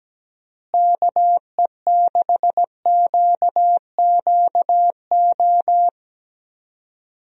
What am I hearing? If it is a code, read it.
Morse code, heard as a keyed tone, about 17 wpm, 700 Hz. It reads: KE6QQO
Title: USA Call Signs - 6 Characters - Repeated 3x - 17wpm